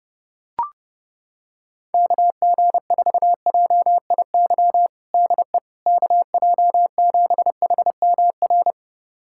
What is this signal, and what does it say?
Morse code, heard as a keyed tone, about 30 wpm, 700 Hz. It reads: KG4JIY DE KJ7HMR
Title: USA Callsign DE Callsign 30wpm